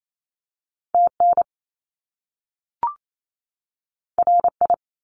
Morse code, heard as a keyed tone, about 28 wpm, 700 Hz. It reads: TN RI
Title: US States 28wpm